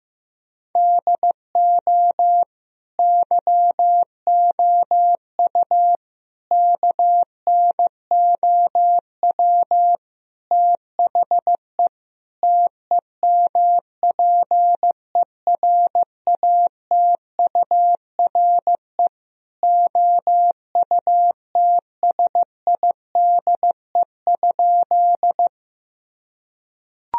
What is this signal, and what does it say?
Morse code, heard as a keyed tone, about 16 wpm, 700 Hz. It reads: DO YOU KNOW THE TEMPERATURE OUTSIDE?
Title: Sentences from Top 1000 Words - Review Part 4 - 15wpm